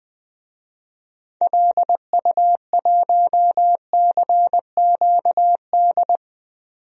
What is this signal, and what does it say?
Morse code, heard as a keyed tone, about 20 wpm, 700 Hz. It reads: LU1CQD
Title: DXCC-100 Call Signs - 6 Characters - 20wpm